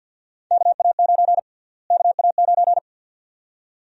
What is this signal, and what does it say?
Morse code, heard as a keyed tone, about 51 wpm, 700 Hz. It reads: KA9 KA9
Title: USA Call Sign Prefixes - Repeated 2x - 50wpm